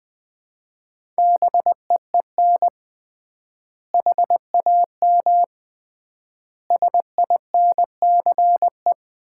Morse code, heard as a keyed tone, about 21 wpm, 700 Hz. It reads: BEEN HAM SINCE
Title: Common QSO Phrases - 3x Word-Spacing - 20wpm